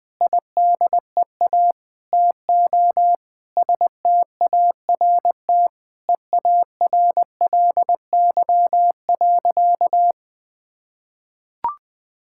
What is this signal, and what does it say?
Morse code, heard as a keyed tone, about 21 wpm, 700 Hz. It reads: IDEA TO START EARLY.
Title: Sentences with 8-Word Phrases - Spoken Beginning - 20wpm